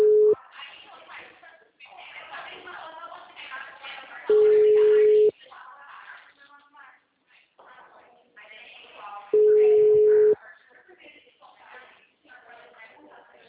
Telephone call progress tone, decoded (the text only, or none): ringback tone